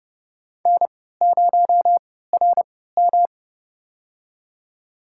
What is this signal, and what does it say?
Morse code, heard as a keyed tone, about 32 wpm, spaced as for 17 wpm, 700 Hz. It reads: N0RM
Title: USA Call Signs - 4 Characters - 3x Character Spacing - 30wpm